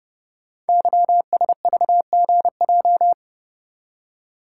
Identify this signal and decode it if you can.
Morse code, heard as a keyed tone, about 30 wpm, 700 Hz. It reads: YSVGJ
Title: Code Groups - 5 Letters - 30wpm